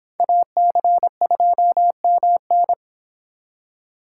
Morse code, heard as a keyed tone, about 25 wpm, 700 Hz. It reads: AC2MN